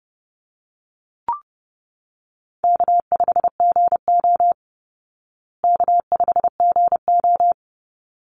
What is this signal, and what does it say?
Morse code, heard as a keyed tone, about 30 wpm, 700 Hz. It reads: K5GO K5GO
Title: Easy Call Signs Repeated 2x - 30wpm